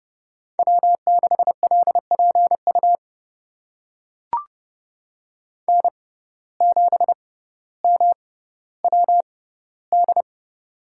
Morse code, heard as a keyed tone, about 30 wpm, spaced as for 15 wpm, 700 Hz. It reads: W6LPU N7MWD
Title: USA Call Signs - 5 Characters - 6x Character Spacing - 30wpm